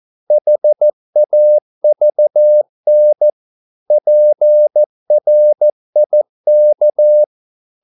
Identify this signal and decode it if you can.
Morse code, heard as a keyed tone, about 14 wpm, 600 Hz. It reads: HAVN PRIK
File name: Korte ord DK 14wpm